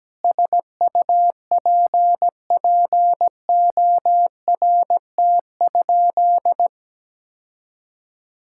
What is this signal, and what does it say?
Morse code, heard as a keyed tone, about 17 wpm, 700 Hz. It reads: SUPPORT?
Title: Sentences from Top 1000 Words - Review Part 5 - 17wpm